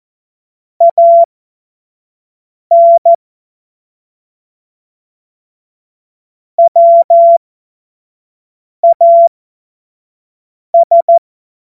Morse code, heard as a keyed tone, about 14 wpm, spaced as for 5 wpm, 680 Hz. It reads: AN WAS
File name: Morsekurs / Witze / Witz 13 Februar 2024 25 F70